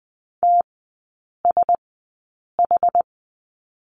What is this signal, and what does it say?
Morse code, heard as a keyed tone, about 20 wpm, spaced as for 8 wpm, 700 Hz. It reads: TSH
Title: 081 Lesson 14e - Number 2 - Three Characters 20wpm